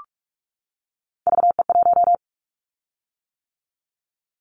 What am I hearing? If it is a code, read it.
Morse code, heard as a keyed tone, about 44 wpm, 700 Hz. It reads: VE1
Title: Callsign Prefixes for Top 25 DXCC - 45wpm